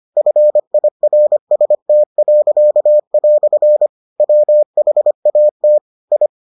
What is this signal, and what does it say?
Morse code, heard as a keyed tone, about 25 wpm, 600 Hz. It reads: FIRST." WHAT I